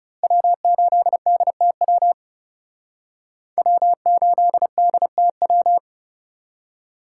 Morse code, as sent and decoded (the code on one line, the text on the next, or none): .-- ---.. -.. - .-- / .-- ---.. -.. - .--
W8DTW W8DTW